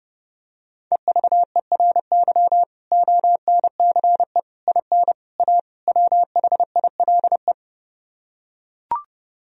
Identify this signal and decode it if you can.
Morse code, heard as a keyed tone, about 32 wpm, 700 Hz. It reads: EVERY ONCE IN A WHILE